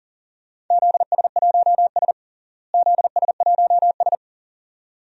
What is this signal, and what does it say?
Morse code, heard as a keyed tone, about 40 wpm, 700 Hz. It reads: ZS1S ZS1S
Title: DXCC-100 Call Signs - 4 Characters - Repeated 2x - 40wpm